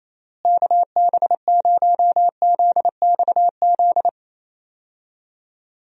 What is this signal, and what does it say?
Morse code, heard as a keyed tone, about 27 wpm, 700 Hz. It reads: KB0ZXZ
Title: USA Call Signs - 6 Characters - Repeated 3x - 28wpm